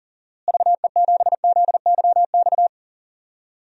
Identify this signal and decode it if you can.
Morse code, heard as a keyed tone, about 40 wpm, 700 Hz. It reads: VE7ZYX
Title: DXCC-100 Call Signs - 6 Characters - Repeated 2x - 40wpm